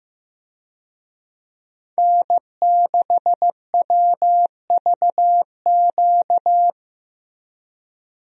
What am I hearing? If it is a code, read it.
Morse code, heard as a keyed tone, about 15 wpm, 700 Hz. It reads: N6WVQ
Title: USA Call Signs - 5 Characters - Repeated 3x - 15wpm